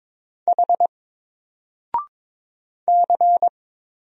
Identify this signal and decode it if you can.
Morse code, heard as a keyed tone, about 22 wpm, 700 Hz. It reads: H C
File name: Single-Letter-Number-22wpm